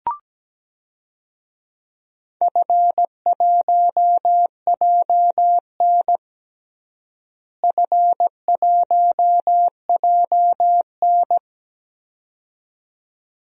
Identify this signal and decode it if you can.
Morse code, heard as a keyed tone, about 17 wpm, 700 Hz. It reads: F1JN F1JN
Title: DXCC-100 Call Signs - 4 Characters - Repeated 2x - 17wpm